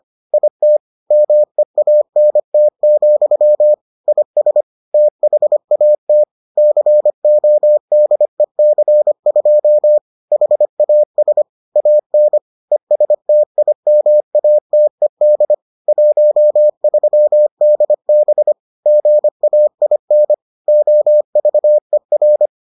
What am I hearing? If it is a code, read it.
Morse code, heard as a keyed tone, about 25 wpm, 600 Hz. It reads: IT MEANT, IS THAT CODEC2 HAS AN ESTIMATED 13DB GAIN OVER